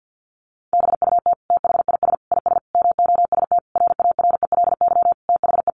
Morse code, heard as a keyed tone, about 50 wpm, 700 Hz. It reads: BUT THIS IS MOST RARELY THE
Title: Sentences from Top 1400 Words 50wpm